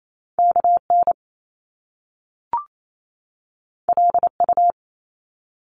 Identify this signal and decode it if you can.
Morse code, heard as a keyed tone, about 28 wpm, 700 Hz. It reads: KN LU
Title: Top Two Letter Combinations in Words 28wpm